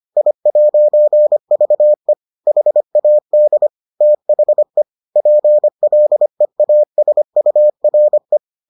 Morse code, heard as a keyed tone, about 25 wpm, 600 Hz. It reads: I'VE HAD THE PLEASURE